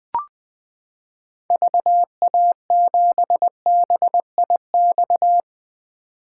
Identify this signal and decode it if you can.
Morse code, heard as a keyed tone, about 20 wpm, 700 Hz. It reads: VA7BIX
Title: 2023 POTA Activator Call Signs - 20wpm